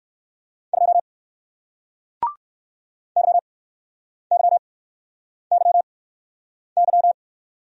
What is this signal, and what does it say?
Morse code, heard as a keyed tone, about 60 wpm, spaced as for 9 wpm, 700 Hz. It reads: 2 YYYY